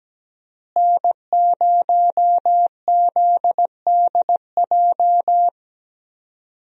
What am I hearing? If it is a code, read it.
Morse code, heard as a keyed tone, about 17 wpm, 700 Hz. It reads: N0ZDJ
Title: USA Call Signs - 5 Characters - Repeated 2x - 17wpm